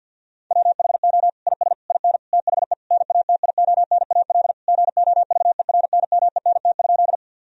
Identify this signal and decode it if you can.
Morse code, heard as a keyed tone, about 50 wpm, 700 Hz. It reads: WHO IS IN THE NATIONAL GOVERNMENT?
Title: Sentences from Top 1000 Words - Review Part 4 - 50wpm